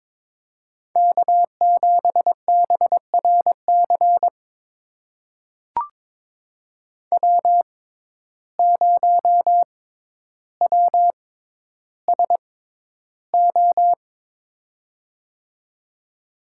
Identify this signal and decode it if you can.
Morse code, heard as a keyed tone, about 22 wpm, spaced as for 11 wpm, 700 Hz. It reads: K7BRC W0WSO
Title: USA Call Signs - 5 Characters - 6x Character Spacing - 22wpm